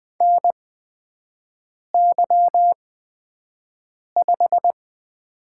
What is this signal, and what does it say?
Morse code, heard as a keyed tone, about 20 wpm, spaced as for 5 wpm, 700 Hz. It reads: NY5
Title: USA Call Signs - 4 Characters - 8x Character Spacing - 20wpm